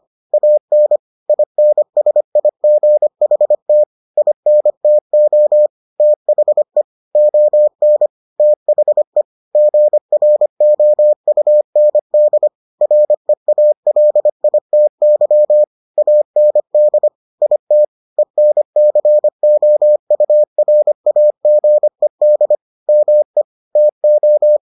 Morse code, heard as a keyed tone, about 25 wpm, 600 Hz. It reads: AN INSIGHT INTO THE ON THE GROUND REALITY AND IT ENCOURAGED ME TO